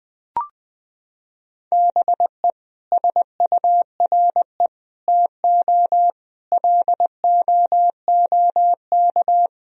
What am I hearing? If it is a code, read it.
Morse code, heard as a keyed tone, about 19 wpm, 700 Hz. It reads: BE SURE TO LOOK
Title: Sentences from Top 1000 Words - Review Part 4 - 20wpm